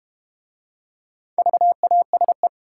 Morse code, heard as a keyed tone, about 30 wpm, 700 Hz. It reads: VASE
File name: practice / 25wpm / Difficult Words - Speed-Racing with Hints - 25wpm